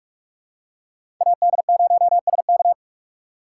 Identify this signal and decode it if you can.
Morse code, heard as a keyed tone, about 45 wpm, 700 Hz. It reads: AD0SK
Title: USA Call Signs - 2x Character Spacing - 45wpm